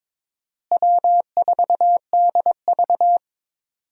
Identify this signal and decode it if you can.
Morse code, heard as a keyed tone, about 22 wpm, 700 Hz. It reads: W4DV
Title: USA Call Signs - 4 Characters - Repeated 2x - 22wpm